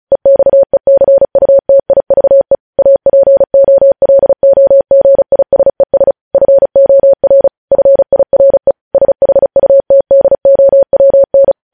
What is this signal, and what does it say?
Morse code, heard as a keyed tone, about 36 wpm, 550 Hz. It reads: EXECUTIVE APOLOGISES FOR FIRE SHUTDOWN
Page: News Headlines in Morse Code at 35 WPM